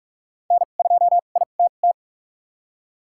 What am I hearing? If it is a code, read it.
Morse code, heard as a keyed tone, about 46 wpm, spaced as for 32 wpm, 700 Hz. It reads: N2ITT
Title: USA Call Signs - 5 Characters - 2x Character Spacing - 45wpm